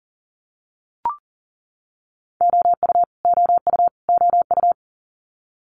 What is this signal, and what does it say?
Morse code, heard as a keyed tone, about 39 wpm, 700 Hz. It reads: OU OU OU
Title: Multi-letter Phonemes Repeated 3x 40wpm